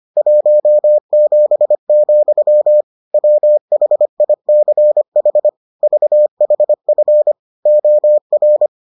Morse code, heard as a keyed tone, about 25 wpm, 600 Hz. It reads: 17, WHICH VHF OR